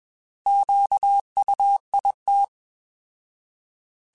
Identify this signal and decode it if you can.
Morse code, heard as a keyed tone, about 21 wpm, 800 Hz. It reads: QUIT
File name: Alphabetical / Alphabetical mp3s / Qs @20